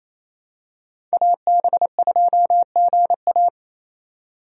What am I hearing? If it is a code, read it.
Morse code, heard as a keyed tone, about 28 wpm, 700 Hz. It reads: AB2GA
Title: USA Call Signs - 5 Characters - 28wpm